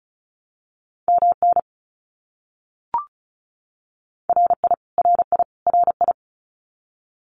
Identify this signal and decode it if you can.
Morse code, heard as a keyed tone, about 34 wpm, 700 Hz. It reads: MN RI RI RI
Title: US States Repeated 3x 35wpm